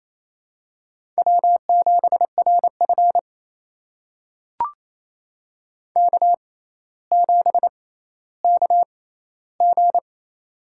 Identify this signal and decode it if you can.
Morse code, heard as a keyed tone, about 28 wpm, spaced as for 14 wpm, 700 Hz. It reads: W7RF K7KG
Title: USA Call Signs - 6x Character Spacing - 28wpm